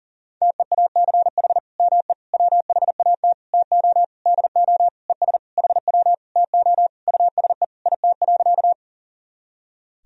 Morse code, heard as a keyed tone, about 40 wpm, 700 Hz. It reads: TEACH ME WHAT TO DO ES HW TO USE IT.
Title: Sentences from Top 500 Words and Abbreviations 40wpm